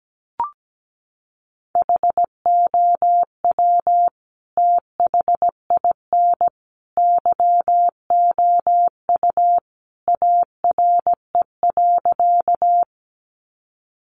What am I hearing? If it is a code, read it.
Morse code, heard as a keyed tone, about 17 wpm, 700 Hz. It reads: HOW THIN YOU ARE.
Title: Sentences from Top 1200 Words 17wpm